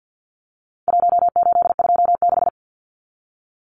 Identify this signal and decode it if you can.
Morse code, heard as a keyed tone, about 48 wpm, 700 Hz. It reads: 1826
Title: Numbers - Contest Style - 4-Digit 50wpm